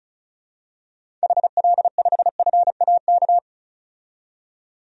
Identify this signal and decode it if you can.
Morse code, heard as a keyed tone, about 36 wpm, 700 Hz. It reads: HL5FAK
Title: DXCC-100 Call Signs - 6 Characters - 35wpm